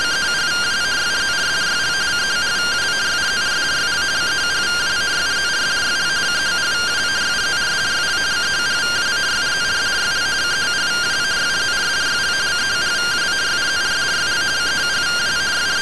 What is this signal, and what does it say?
Two-tone teleprinter signal, about 170 Hz shift.